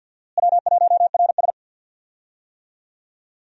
Morse code, heard as a keyed tone, about 49 wpm, 700 Hz. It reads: W1RS
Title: USA Call Signs - 4 Characters - Repeated 3x - 50wpm